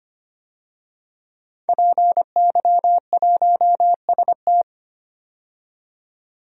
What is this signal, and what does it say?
Morse code, heard as a keyed tone, about 25 wpm, 700 Hz. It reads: PY1ST
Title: DXCC-100 Call Signs - 5 Characters - 25wpm